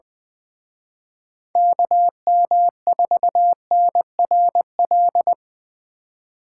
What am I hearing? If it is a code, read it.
Morse code, heard as a keyed tone, about 23 wpm, 700 Hz. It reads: KM4NRL
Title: Speed Racing - USA 2x3 Call Signs 20wpm